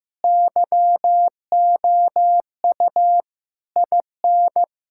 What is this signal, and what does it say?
Morse code, heard as a keyed tone, about 15 wpm, 700 Hz. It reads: YOU IN
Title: Sentences from Top 1600 Words 15wpm